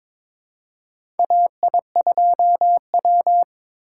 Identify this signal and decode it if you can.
Morse code, heard as a keyed tone, about 23 wpm, 700 Hz. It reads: AI2W
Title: USA Call Signs - 4 Characters - 3x Character Spacing - 22wpm